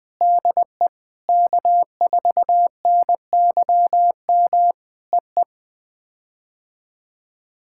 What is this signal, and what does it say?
Morse code, heard as a keyed tone, about 20 wpm, 700 Hz. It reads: DE K4NYM EE
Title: POTA QSO Practice - 20wpm